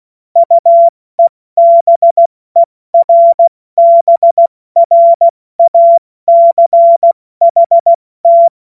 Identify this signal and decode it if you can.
Morse code, heard as a keyed tone, about 16 wpm, 680 Hz. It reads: UEBERBRACHT
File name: Morsekurs / Witze / Witz 13. Juni 2023 70 F80